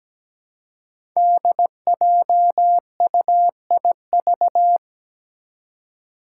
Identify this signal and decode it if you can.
Morse code, heard as a keyed tone, about 17 wpm, 700 Hz. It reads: DJUIV
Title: Five Challenging Letter-Number Combinations 17wpm